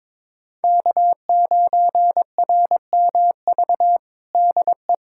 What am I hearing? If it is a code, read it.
Morse code, heard as a keyed tone, about 22 wpm, 700 Hz. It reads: K9RMV DE
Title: USA Callsign DE Callsign - 5 Characters - 22wpm